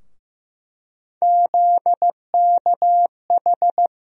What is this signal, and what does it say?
Morse code, heard as a keyed tone, about 15 wpm, 700 Hz. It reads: ZKH
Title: USA Call Sign Postfixes - 3 Letters 15wpm